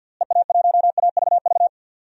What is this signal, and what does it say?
Morse code, heard as a keyed tone, about 48 wpm, 700 Hz. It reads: EA1AVV
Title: DXCC-100 Call Signs - 6 Characters - Repeated 3x - 50wpm